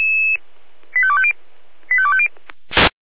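Five-tone selective call (ZVEI), heard as quick runs of five tones, tones of about 70 ms. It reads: 85250 85250
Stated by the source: Five-tone ZVEI selective call received on an SDR.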